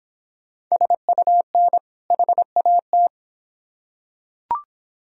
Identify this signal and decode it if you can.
Morse code, heard as a keyed tone, about 25 wpm, 700 Hz. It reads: SUN HAT